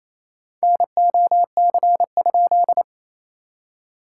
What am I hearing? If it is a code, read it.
Morse code, heard as a keyed tone, about 28 wpm, 700 Hz. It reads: NOC?
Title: SOTA QSO Practice 28wpm